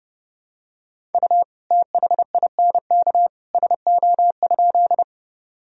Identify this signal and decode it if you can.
Morse code, heard as a keyed tone, about 30 wpm, 700 Hz. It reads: U THINK SO?
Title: Sentences from Top 200 Words and Abbreviations 30wpm